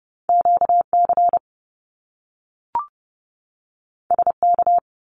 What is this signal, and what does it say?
Morse code, heard as a keyed tone, about 30 wpm, 700 Hz. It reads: QC SK
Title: Canadian Provinces and Territories 30wpm